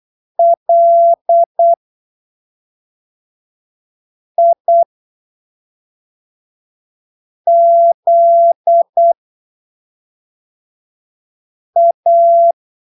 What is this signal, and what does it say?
Morse code, heard as a keyed tone, about 8 wpm, spaced as for 3 wpm, 680 Hz. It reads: LIZA
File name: arquivos / texto / texto 07 08ppm